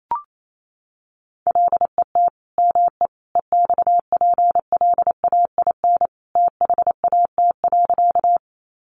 Morse code, heard as a keyed tone, about 28 wpm, 700 Hz. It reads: LET ME EXPLAIN THAT.
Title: Sentences from Top 1100 Words 28wpm